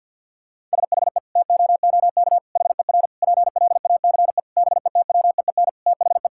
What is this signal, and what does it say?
Morse code, heard as a keyed tone, about 50 wpm, 700 Hz. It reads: SHE TOOK HER PLACE BETWEEN THE
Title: Sentences from Top 1000 Words - Review Part 5 - 50wpm